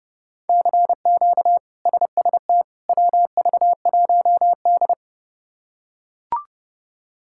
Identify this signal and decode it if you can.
Morse code, heard as a keyed tone, about 30 wpm, 700 Hz. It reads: CQ SST WV1D
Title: K1USN SST QSO Practice - 30wpm